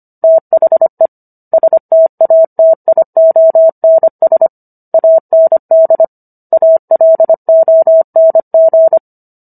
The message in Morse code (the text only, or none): THE STATIONS AND ALONG